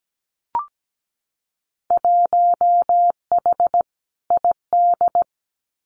Morse code, heard as a keyed tone, about 17 wpm, 700 Hz. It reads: 1H ID
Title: Winter Field Day Exchanges 17wpm